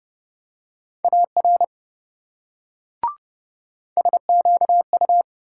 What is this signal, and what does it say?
Morse code, heard as a keyed tone, about 26 wpm, 700 Hz. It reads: AR SQU